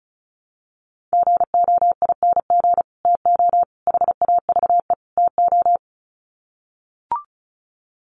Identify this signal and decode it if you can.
Morse code, heard as a keyed tone, about 34 wpm, 700 Hz. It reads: GOING TO HAVE TO
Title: Sets of 4 Words 35wpm